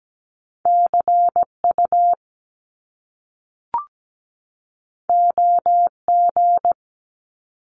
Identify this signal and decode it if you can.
Morse code, heard as a keyed tone, about 17 wpm, 700 Hz. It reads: CU OG
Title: Top Two Letter Combinations in Words 17wpm